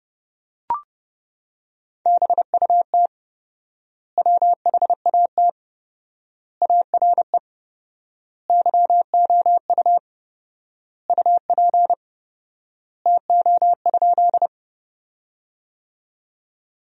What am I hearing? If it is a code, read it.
Morse code, heard as a keyed tone, about 32 wpm, 700 Hz. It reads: BUT WHAT ARE YOU UP TO?
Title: 4x Word-Spacing - Sentences from Top 100 Words - 30wpm